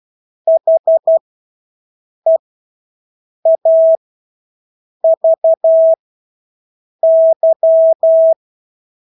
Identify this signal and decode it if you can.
Morse code, heard as a keyed tone, about 12 wpm, spaced as for 6 wpm, 650 Hz. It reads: HEAVY
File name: Engelske ord 12-6 wpm